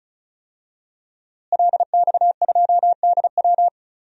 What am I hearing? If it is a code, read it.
Morse code, heard as a keyed tone, about 36 wpm, 700 Hz. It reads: LX2DW
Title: DXCC-100 Call Signs - 5 Characters - Repeated 2x - 35wpm